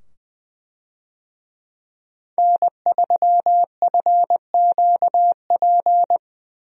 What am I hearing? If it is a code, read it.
Morse code, heard as a keyed tone, about 20 wpm, 700 Hz. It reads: N3FQP
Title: USA Call Signs - 5 Characters - 20wpm